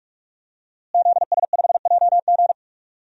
Morse code, heard as a keyed tone, about 46 wpm, 700 Hz. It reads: ZS5JG
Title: DXCC-100 Call Signs - 5 Characters - Repeated 3x - 45wpm